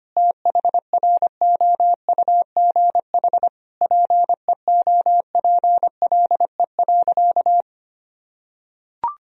Morse code, heard as a keyed tone, about 25 wpm, 700 Hz. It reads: THROUGH PEOPLE.
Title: Sentences from Top 200 Words and Abbreviations 25wpm